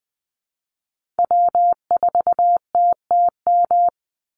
Morse code, heard as a keyed tone, about 20 wpm, 700 Hz. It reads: W4TTM
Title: Call Signs Repeated 3x - 20wpm